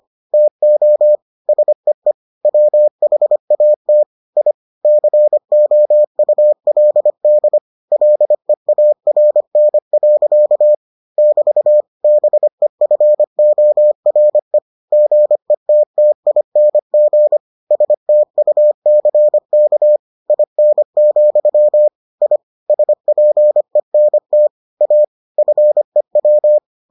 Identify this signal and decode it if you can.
Morse code, heard as a keyed tone, about 25 wpm, 600 Hz. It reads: TO SEE WHAT I COULD LEARN. <BT> BEFORE GETTING STUCK IN, I SPENT A FEW